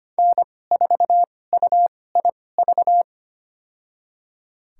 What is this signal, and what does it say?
Morse code, heard as a keyed tone, about 25 wpm, spaced as for 18 wpm, 700 Hz. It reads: N4UIV
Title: USA Call Signs - 5 Characters - 2x Character Spacing - 25wpm